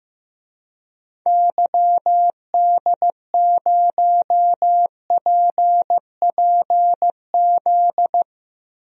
Morse code, heard as a keyed tone, about 15 wpm, 700 Hz. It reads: YD0PPZ
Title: DXCC-100 Call Signs - 6 Characters - Repeated 3x - 15wpm